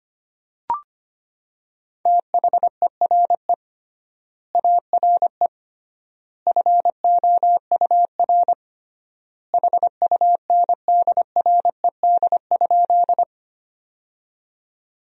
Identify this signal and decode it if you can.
Morse code, heard as a keyed tone, about 25 wpm, 700 Hz. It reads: THERE ARE FOUR HUNDRED?
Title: 3x Word-Spacing - Sentences from Top 400 Words - 25wpm